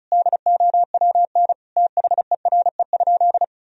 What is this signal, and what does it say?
Morse code, heard as a keyed tone, about 36 wpm, 700 Hz. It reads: DOWN THERE?
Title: Sentences from Top 1000 Words - Review Part 4 - 35wpm